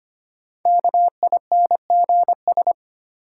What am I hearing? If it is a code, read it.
Morse code, heard as a keyed tone, about 25 wpm, 700 Hz. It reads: KINGS
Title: Sentences from Top 1000 Words - Review Part 6 - New Word Drill - 25wpm